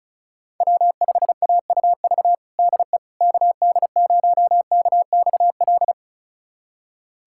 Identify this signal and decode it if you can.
Morse code, heard as a keyed tone, about 34 wpm, 700 Hz. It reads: W5AUV DE KD0KXL
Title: USA Callsign DE Callsign 35wpm